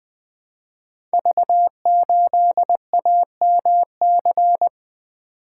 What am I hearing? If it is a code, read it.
Morse code, heard as a keyed tone, about 20 wpm, 700 Hz. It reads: V8AMC
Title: Code Groups - 5 Letter:Numbers - 20wpm